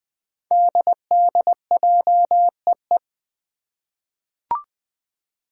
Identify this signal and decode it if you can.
Morse code, heard as a keyed tone, about 20 wpm, 700 Hz. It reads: DDJEE
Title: Five Challenging Letter-Number Combinations 20wpm